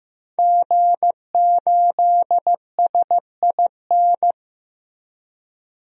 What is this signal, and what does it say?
Morse code, heard as a keyed tone, about 15 wpm, 700 Hz. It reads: G8SIN
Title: DXCC-100 Call Signs - 5 Characters - 15wpm